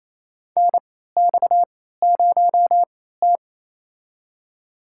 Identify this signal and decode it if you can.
Morse code, heard as a keyed tone, about 28 wpm, spaced as for 16 wpm, 700 Hz. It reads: NX0T
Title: USA Call Signs - 4 Characters - 3x Character Spacing - 28wpm